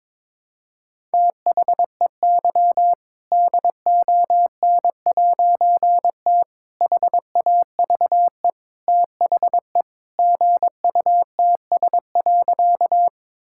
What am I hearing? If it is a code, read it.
Morse code, heard as a keyed tone, about 22 wpm, 700 Hz. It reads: THEY DON'T HAVE THE GUTS.
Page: morse-code-ninja-practice-22wpm